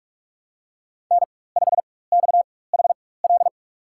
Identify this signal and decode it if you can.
Morse code, heard as a keyed tone, about 45 wpm, spaced as for 21 wpm, 700 Hz. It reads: N5XHL
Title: USA Call Signs - 4x Character Spacing - 45wpm